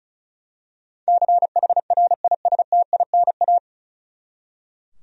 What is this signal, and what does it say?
Morse code, heard as a keyed tone, about 36 wpm, 700 Hz. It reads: CHRISTINA
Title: USA Common Names - 35wpm